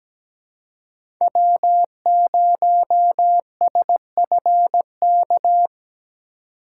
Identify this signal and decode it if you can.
Morse code, heard as a keyed tone, about 17 wpm, 700 Hz. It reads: W0SFK
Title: USA Call Signs - 5 Characters - 2x Character Spacing - 17wpm